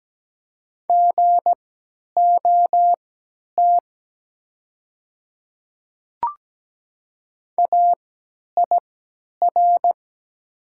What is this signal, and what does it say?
Morse code, heard as a keyed tone, about 17 wpm, spaced as for 10 wpm, 700 Hz. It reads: GOT AIR
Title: 3x Character Spacing - Top 300 Words - 17wpm